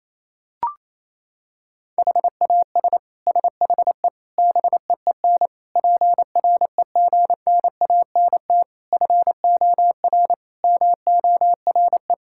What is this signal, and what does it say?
Morse code, heard as a keyed tone, about 28 wpm, 700 Hz. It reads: HAS SHE BEEN PREGNANT FOR MORE
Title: Sentences from Top 1000 Words - Review Part 6 - 28wpm